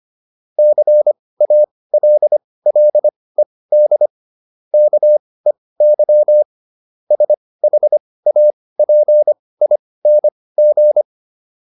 Morse code, heard as a keyed tone, about 25 wpm, spaced as for 18 wpm, 600 Hz. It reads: CALLED KEY SHAPING